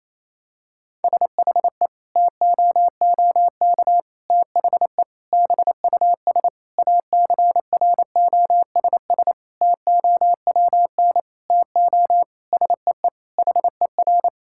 Morse code, heard as a keyed tone, about 28 wpm, 690 Hz. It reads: SHE TOOK THE BUS ACROSS TOWN TO SEE HER
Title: Sentences from Top 1000 Words - Review Part 6 - 28wpm